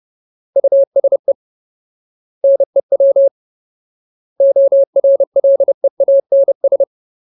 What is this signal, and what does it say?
Morse code, heard as a keyed tone, about 30 wpm, 550 Hz. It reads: USE NEW ORLEANS
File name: Ultimate Word List / 30WPM Ultimate